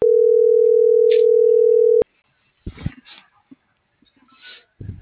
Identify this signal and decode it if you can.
Ringback tone (North America), 440 + 480 Hz.